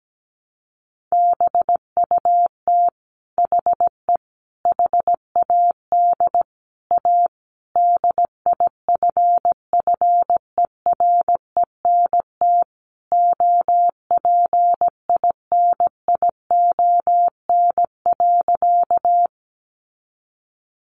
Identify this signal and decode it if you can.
Morse code, heard as a keyed tone, about 17 wpm, 700 Hz. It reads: BUT HE HAD A DIFFERENT OPINION.
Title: Sentences from Top 1200 Words 17wpm